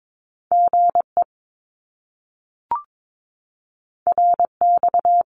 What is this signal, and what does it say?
Morse code, heard as a keyed tone, about 22 wpm, 700 Hz. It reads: GE RX
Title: Abbreviations Repeated 3x - 22wpm